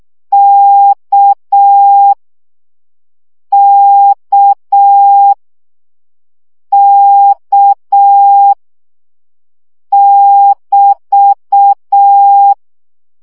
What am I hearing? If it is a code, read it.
Morse code, heard as a keyed tone, about 6 wpm, 800 Hz. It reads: K K K <BT>
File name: CW gyakorlas k 5 baud